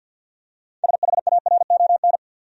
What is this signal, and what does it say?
Morse code, heard as a keyed tone, about 53 wpm, 700 Hz. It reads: SHARON
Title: USA Common Names - 50wpm